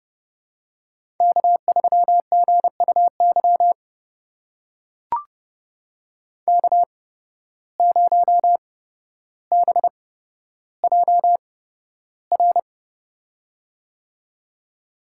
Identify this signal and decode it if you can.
Morse code, heard as a keyed tone, about 31 wpm, spaced as for 13 wpm, 700 Hz. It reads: K3GUY K0BJR
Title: USA Call Signs - 5 Characters - 8x Character Spacing - 30wpm